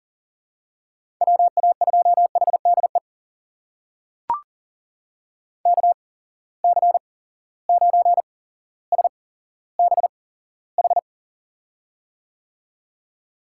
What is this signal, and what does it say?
Morse code, heard as a keyed tone, about 39 wpm, spaced as for 17 wpm, 700 Hz. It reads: WA2HDE KC9SBH
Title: USA Call Signs - 6 Characters - 8x Character Spacing - 40wpm